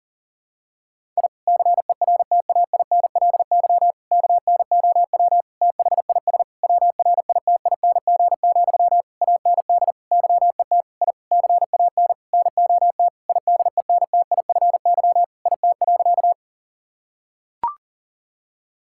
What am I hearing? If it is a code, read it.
Morse code, heard as a keyed tone, about 40 wpm, 700 Hz. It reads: I CERTAINLY KNOW THIS WRITING, AND YET I CAN NOT IDENTIFY IT.
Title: Sentences from Top 1000 Words - Review Part 6 - 40wpm